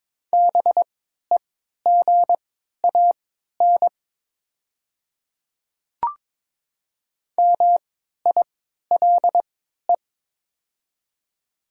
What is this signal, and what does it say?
Morse code, heard as a keyed tone, about 22 wpm, spaced as for 12 wpm, 700 Hz. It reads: BEGAN MILE
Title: 3x Character Spacing - Top 300 Words - 22wpm